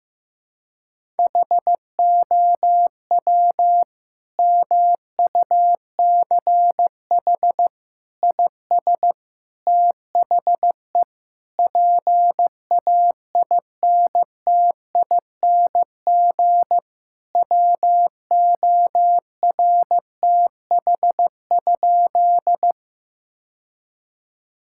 Morse code, heard as a keyed tone, about 15 wpm, 700 Hz. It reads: HOW MUCH IS THE PAINTING WORTH?
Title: Sentences from Top 1000 Words - Review Part 6 - 15wpm